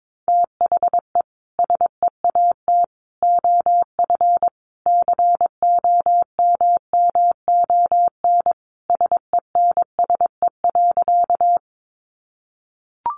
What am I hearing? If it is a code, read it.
Morse code, heard as a keyed tone, about 22 wpm, 700 Hz. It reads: THE SEAT OF COMMON SENSE.
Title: Sentences from Top 1500 Words 22wpm